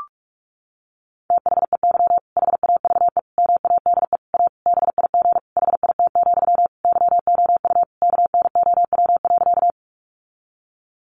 Morse code, heard as a keyed tone, about 44 wpm, 700 Hz. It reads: THEY HAVE MADE A BIG HIT, YOU KNOW.
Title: Sentences from Top 1200 Words 45wpm